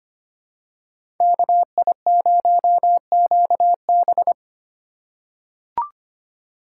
Morse code, heard as a keyed tone, about 25 wpm, 700 Hz. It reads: KI0QB
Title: USA Call Signs - 6x Character Spacing - 25wpm